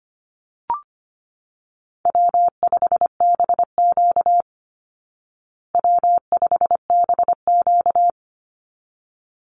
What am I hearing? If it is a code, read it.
Morse code, heard as a keyed tone, about 25 wpm, 700 Hz. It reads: W5BQ W5BQ
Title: Call Signs Repeated 3x - 25wpm